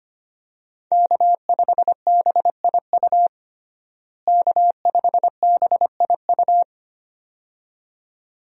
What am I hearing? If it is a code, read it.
Morse code, heard as a keyed tone, about 25 wpm, 700 Hz. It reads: K5BIU K5BIU
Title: USA Call Signs - 5 Characters - Repeated 2x - 25wpm